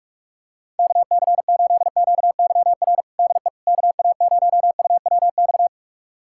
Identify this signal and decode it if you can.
Morse code, heard as a keyed tone, about 46 wpm, 690 Hz. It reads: KC8QYR DE KA0UWX
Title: USA Callsign DE Callsign - 6 Characters - 45wpm